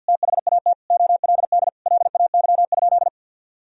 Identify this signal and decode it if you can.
Morse code, heard as a keyed tone, about 51 wpm, 700 Hz. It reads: THAT OLD LAY?
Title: Sentences from Top 400 Words and Abbreviations 50wpm